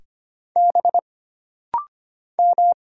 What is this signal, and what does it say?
Morse code, heard as a keyed tone, about 25 wpm, 700 Hz. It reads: B M
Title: Single Letters - Rapid-Fire 25wpm